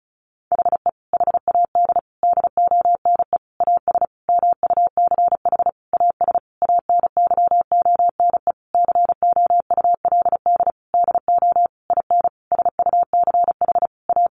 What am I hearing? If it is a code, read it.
Morse code, heard as a keyed tone, about 34 wpm, 700 Hz. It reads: HE HAD DONE AS MUCH AS ANYONE COULD DO IN SUCH A
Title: Sentences from Top 1200 Words 35wpm